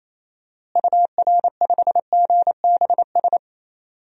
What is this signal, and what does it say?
Morse code, heard as a keyed tone, about 28 wpm, 700 Hz. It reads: UR5GBS
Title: DXCC-100 Call Signs - 6 Characters - Repeated 2x - 28wpm